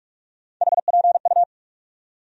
Morse code, heard as a keyed tone, about 46 wpm, 700 Hz. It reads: HPU
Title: USA Call Sign Postfixes - 3 Letters 45wpm